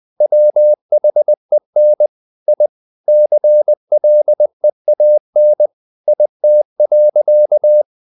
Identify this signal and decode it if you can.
Morse code, heard as a keyed tone, about 20 wpm, 600 Hz. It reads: WHEN I CLEAN IT.
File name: the old man and the sea part 2 20wpm